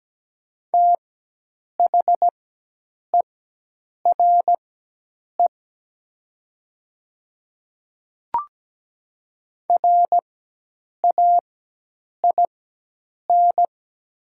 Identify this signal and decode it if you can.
Morse code, heard as a keyed tone, about 17 wpm, spaced as for 8 wpm, 700 Hz. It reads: THERE RAIN
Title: 4x Character Spacing - Top 300 Words - 17wpm